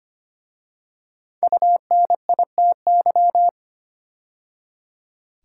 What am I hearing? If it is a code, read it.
Morse code, heard as a keyed tone, about 25 wpm, 700 Hz. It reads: UNITY
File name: 5 Letter Words - First 3 Letters Given - 25wpm